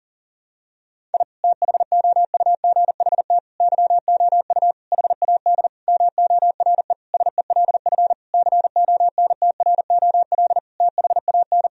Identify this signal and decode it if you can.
Morse code, heard as a keyed tone, about 40 wpm, 700 Hz. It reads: I THOUGHT YOU HAD MORE SELF CONTROL THAN
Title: Sentences from Top 1000 Words - Review Part 3 - 40wpm